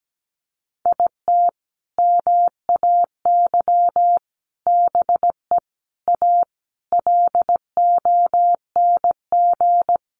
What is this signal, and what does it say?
Morse code, heard as a keyed tone, about 17 wpm, 700 Hz. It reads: IT MAY BE A LONG
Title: Sentences from Top 1100 Words 17wpm